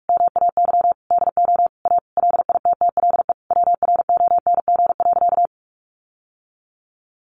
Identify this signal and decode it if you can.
Morse code, heard as a keyed tone, about 44 wpm, 700 Hz. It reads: MAY DO A LITTLE WRONG.
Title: Sentences from Top 900 Words 45wpm